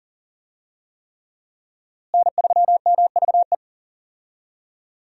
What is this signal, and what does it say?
Morse code, heard as a keyed tone, about 40 wpm, 700 Hz. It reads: N3MVE
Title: USA Call Signs - 5 Characters - Repeated 3x - 40wpm